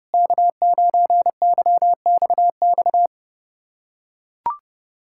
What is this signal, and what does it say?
Morse code, heard as a keyed tone, about 32 wpm, 700 Hz. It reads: K9YXX
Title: USA Callsign DE Callsign - 5 Characters - 30wpm